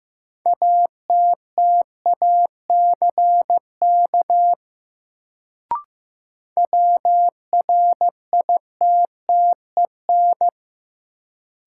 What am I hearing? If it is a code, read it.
Morse code, heard as a keyed tone, about 15 wpm, 700 Hz. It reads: ATTACK WRITTEN
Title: Sentences from Top 1500 Words - New Word Drill - 15wpm